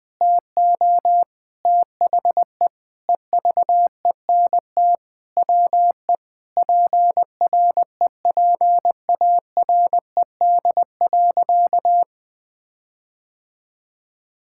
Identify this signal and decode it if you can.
Morse code, heard as a keyed tone, about 20 wpm, 700 Hz. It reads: TO THE EVENT WE PREPARED.
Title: Sentences with 6-Word Phrases - 20wpm